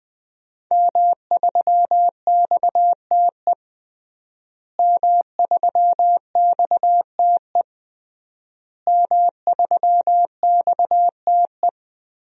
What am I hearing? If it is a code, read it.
Morse code, heard as a keyed tone, about 20 wpm, 700 Hz. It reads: M3XTE M3XTE M3XTE
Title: DXCC-100 Call Signs - 5 Characters - Repeated 3x - 20wpm